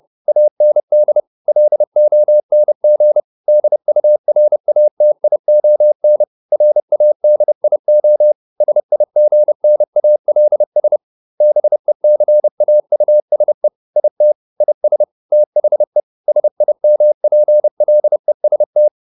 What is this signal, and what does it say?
Morse code, heard as a keyed tone, about 30 wpm, 600 Hz. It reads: AND LONG DURATION RADIO SIGNALS BECAUSE IT IS THE SIMPLEST